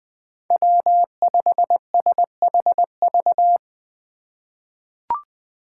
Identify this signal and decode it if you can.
Morse code, heard as a keyed tone, about 20 wpm, 700 Hz. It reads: W5SHV
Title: USA Call Signs - 2x Character Spacing - 20wpm